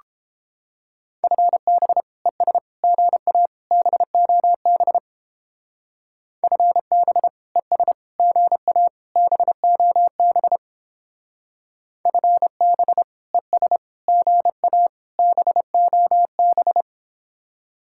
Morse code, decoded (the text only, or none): FB ES GA BOB FB ES GA BOB FB ES GA BOB